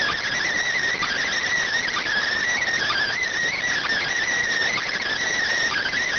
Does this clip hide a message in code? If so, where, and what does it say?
Yes: Emergency Alert System (SAME) tones from 1.0 to 5.0 s, heard as harsh buzzing bursts.